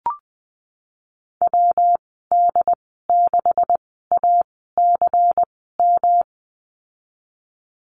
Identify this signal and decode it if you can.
Morse code, heard as a keyed tone, about 20 wpm, spaced as for 14 wpm, 700 Hz. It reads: WD6ACM
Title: 140 Lesson 23g - Letter f - Call Signs 20wpm